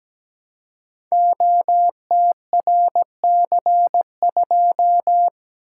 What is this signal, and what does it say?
Morse code, heard as a keyed tone, about 17 wpm, 700 Hz. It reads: OTRC2
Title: Code Groups - 5 Letter:Numbers - 17wpm